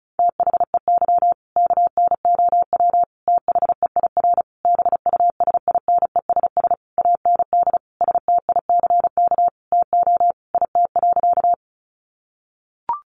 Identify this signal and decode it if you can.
Morse code, heard as a keyed tone, about 34 wpm, 700 Hz. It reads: THEY KNOW THEIR BUSINESS AND STICK TO IT.
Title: Sentences from Top 1200 Words 35wpm